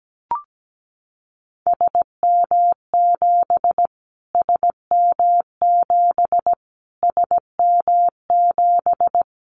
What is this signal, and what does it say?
Morse code, heard as a keyed tone, about 17 wpm, 700 Hz. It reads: SM7 SM7 SM7
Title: Callsign Prefixes for Top 25 DXCC - Repeated 3x - 17wpm